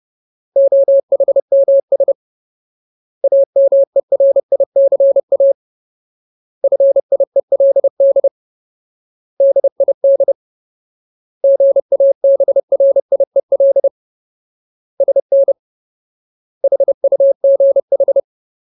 Morse code, heard as a keyed tone, about 30 wpm, 550 Hz. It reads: OHMS AMERICA FIELD DID GABRIEL SN HUGH
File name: Ultimate Word List / 30WPM Ultimate